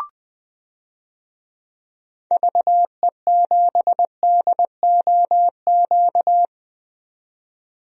Morse code, heard as a keyed tone, about 20 wpm, 700 Hz. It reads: VE7DOQ
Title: DXCC-100 Call Signs - 6 Characters - 20wpm